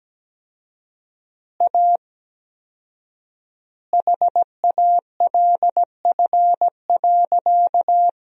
Morse code, heard as a keyed tone, about 17 wpm, 700 Hz. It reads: A HALF.
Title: 4x Word-Spacing - Sentences from Top 400 Words - 17wpm